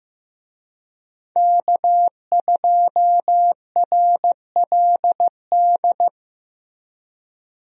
Morse code, heard as a keyed tone, about 15 wpm, 700 Hz. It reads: K2RLD
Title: USA Call Signs - 5 Characters - Repeated 2x - 15wpm